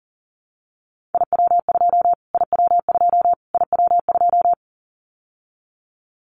Morse code, heard as a keyed tone, about 40 wpm, 700 Hz. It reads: IW2 IW2 IW2
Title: Callsign Prefixes for Top 25 DXCC - Repeated 3x - 40wpm